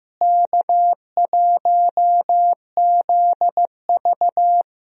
Morse code, heard as a keyed tone, about 15 wpm, 700 Hz. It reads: K1ZV
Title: USA Call Signs - 4 Characters - Repeated 2x - 15wpm